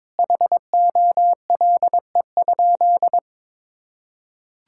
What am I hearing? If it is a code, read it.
Morse code, heard as a keyed tone, about 22 wpm, 700 Hz. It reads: HOLE?
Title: Sentences from Top 1600 Words 22wpm